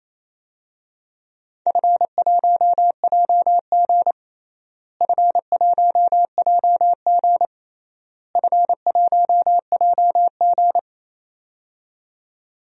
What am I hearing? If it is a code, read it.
Morse code, heard as a keyed tone, about 28 wpm, 700 Hz. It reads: F1JG F1JG F1JG
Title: DXCC-100 Call Signs - 4 Characters - Repeated 3x - 28wpm